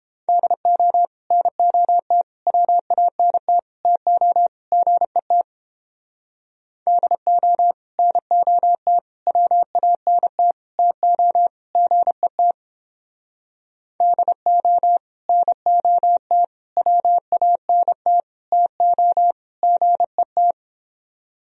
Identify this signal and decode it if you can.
Morse code, heard as a keyed tone, about 29 wpm, 700 Hz. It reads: DO NOT WANT TO GET DO NOT WANT TO GET DO NOT WANT TO GET